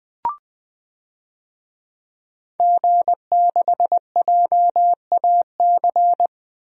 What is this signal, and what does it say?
Morse code, heard as a keyed tone, about 20 wpm, 700 Hz. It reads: G6JAC
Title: DXCC-100 Call Signs - 5 Characters - 20wpm